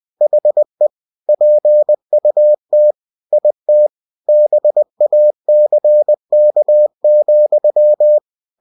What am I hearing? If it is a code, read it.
Morse code, heard as a keyed tone, about 20 wpm, 600 Hz. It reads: HE PUT IT BACK,
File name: cw-books / winnie-the-pooh / winnie-20wpm-page-0014